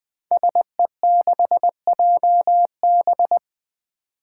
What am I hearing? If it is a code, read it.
Morse code, heard as a keyed tone, about 20 wpm, 700 Hz. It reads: SE6JB
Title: Five Challenging Letter-Number Combinations 20wpm